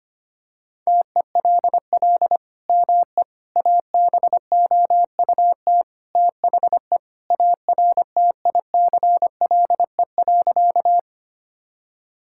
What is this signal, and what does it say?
Morse code, heard as a keyed tone, about 25 wpm, 700 Hz. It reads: TELL ME ABOUT THE ARTICLE.
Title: Sentences from Top 1000 Words - Review Part 5 - 25wpm